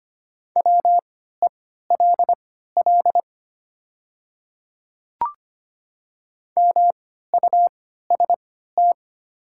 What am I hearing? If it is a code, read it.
Morse code, heard as a keyed tone, about 25 wpm, spaced as for 14 wpm, 700 Hz. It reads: WELL MUST
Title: 3x Character Spacing - Top 300 Words - 25wpm